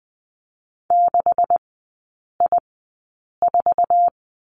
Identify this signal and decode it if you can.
Morse code, heard as a keyed tone, about 20 wpm, spaced as for 8 wpm, 690 Hz. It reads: 6I4
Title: 133 Lesson 22e - Question Mark - Three Characters 20wpm